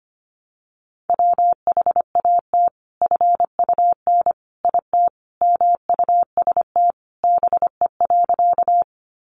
Morse code, heard as a keyed tone, about 25 wpm, 700 Hz. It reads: WHAT FUN IT MUST BE.
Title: Sentences from Top 700 Words 25wpm